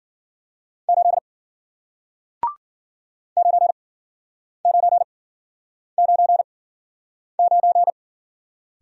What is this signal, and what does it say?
Morse code, heard as a keyed tone, about 53 wpm, spaced as for 9 wpm, 700 Hz. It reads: 8 9999